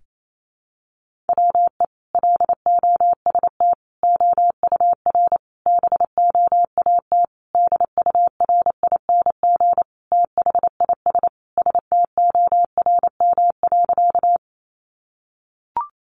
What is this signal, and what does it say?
Morse code, heard as a keyed tone, about 28 wpm, 700 Hz. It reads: WE LOST OUR BOAT DURING THIS STORM.
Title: Sentences from Top 1500 Words 28wpm